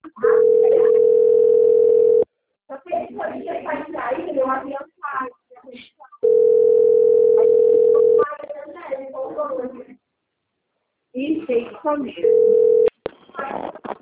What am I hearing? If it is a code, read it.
Ringback tone (North America), 440 + 480 Hz, with speech.
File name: out- - - - - .24478